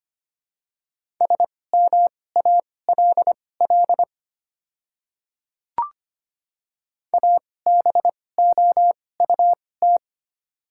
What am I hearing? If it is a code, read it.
Morse code, heard as a keyed tone, about 25 wpm, spaced as for 18 wpm, 700 Hz. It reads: SMALL ABOUT
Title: 2x Character Spacing - Top 300 Words - 25wpm